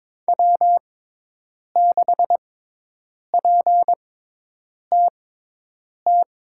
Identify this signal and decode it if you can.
Morse code, heard as a keyed tone, about 22 wpm, spaced as for 8 wpm, 700 Hz. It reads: W6PTT
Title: USA Call Signs - 5 Characters - 6x Character Spacing - 22wpm